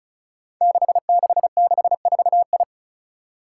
Morse code, heard as a keyed tone, about 36 wpm, 700 Hz. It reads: 6664I
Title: Five Challenging Letter-Number Combinations 35wpm